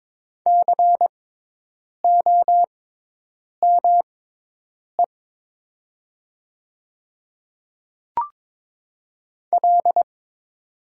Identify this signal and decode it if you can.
Morse code, heard as a keyed tone, about 22 wpm, spaced as for 8 wpm, 700 Hz. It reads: COME L
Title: 6x Character Spacing - Top 100 Words - 22wpm